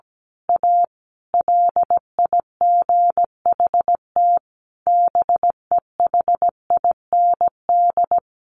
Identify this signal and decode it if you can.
Morse code, heard as a keyed tone, about 17 wpm, 700 Hz. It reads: A LIGHT BEHIND
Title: Sentences from Top 1000 Words 17wpm